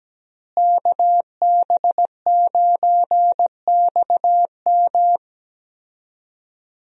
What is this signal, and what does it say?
Morse code, heard as a keyed tone, about 17 wpm, 700 Hz. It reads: KB9XM
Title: USA Call Signs - 5 Characters - Repeated 3x - 17wpm